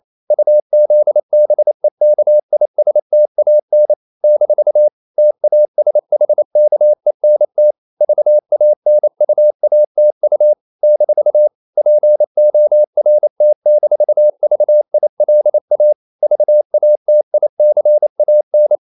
Morse code, heard as a keyed tone, about 28 wpm, 600 Hz. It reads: UZBEKISTAN - TASHKENT VANUATU - PORT-VILA VATICAN
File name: CW / Practice / NationalCapitals28